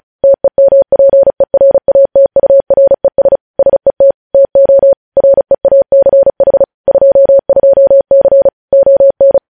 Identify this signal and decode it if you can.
Morse code, heard as a keyed tone, about 36 wpm, 550 Hz. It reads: TEMPERATURES SET TO REACH 22C ON
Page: news headlines in morse code at 35 wpm